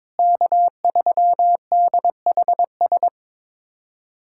Morse code, heard as a keyed tone, about 22 wpm, 700 Hz. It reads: K3DHS
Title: USA Callsign DE Callsign - 5 Characters - 22wpm